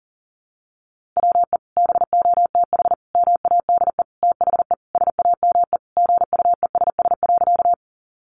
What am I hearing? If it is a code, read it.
Morse code, heard as a keyed tone, about 40 wpm, 700 Hz. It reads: WE BOTH MADE THE SAME GUESS.
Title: Sentences from Top 1500 Words 40wpm